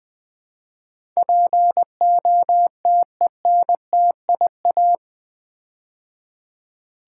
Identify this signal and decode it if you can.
Morse code, heard as a keyed tone, about 20 wpm, 700 Hz. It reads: POTENTIA
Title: Build A Word - 4 Letters At A Time - 11-Letter Words - 20wpm